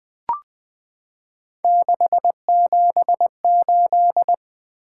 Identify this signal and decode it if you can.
Morse code, heard as a keyed tone, about 21 wpm, 700 Hz. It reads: 678
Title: Three Letter-Number Combinations - 20wpm